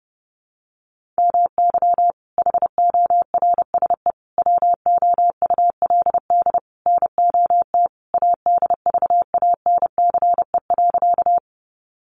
Morse code, heard as a keyed tone, about 30 wpm, 700 Hz. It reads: MY HORSE WOULD NOT ADVANCE.
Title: Sentences from Top 1400 Words 30wpm